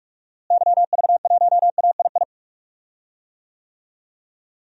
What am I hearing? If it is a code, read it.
Morse code, heard as a keyed tone, about 42 wpm, 700 Hz. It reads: YV1AII
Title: DXCC-100 Call Signs - 6 Characters - Repeated 2x - 45wpm